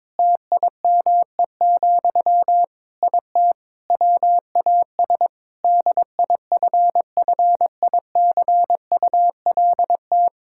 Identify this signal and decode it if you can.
Morse code, heard as a keyed tone, about 22 wpm, 700 Hz. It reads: TIME, IT WAS DIFFICULT
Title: Sentences from Top 1000 Words - Review Part 5 - 22wpm